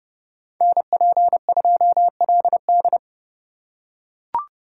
Morse code, heard as a keyed tone, about 24 wpm, 700 Hz. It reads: NP2LD
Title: Speed Racing - USA Call Signs 20wpm